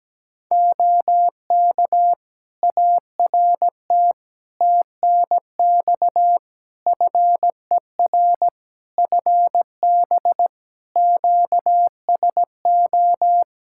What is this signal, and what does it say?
Morse code, heard as a keyed tone, about 17 wpm, 700 Hz. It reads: OK ART TNX FER FB QSO
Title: Practice QSO - Exchange 4 - Ending - 17wpm